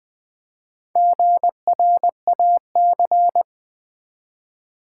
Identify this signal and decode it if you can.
Morse code, heard as a keyed tone, about 20 wpm, 700 Hz. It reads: GRAC
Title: Build A Word - 1 Letter At A Time - 5-Letter Words - 20wpm